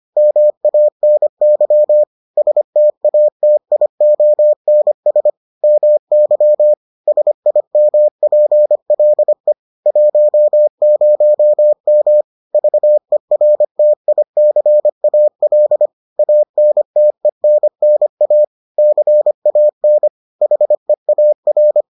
Morse code, heard as a keyed tone, about 25 wpm, 600 Hz. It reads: MANY STATIONS MY SIMPLE 10M VERTICAL ANTENNA CAN HEAR